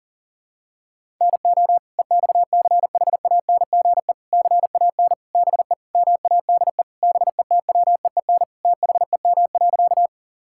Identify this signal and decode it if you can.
Morse code, heard as a keyed tone, about 40 wpm, 700 Hz. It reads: NO EXCHANGE CAN BE MADE BETWEEN THEM.
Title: Sentences from Top 1000 Words - Review Part 6 - 40wpm